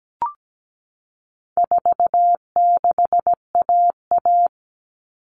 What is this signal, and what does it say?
Morse code, heard as a keyed tone, about 17 wpm, 700 Hz. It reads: 46AA
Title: All Cut Numbers - 4-Digit 17wpm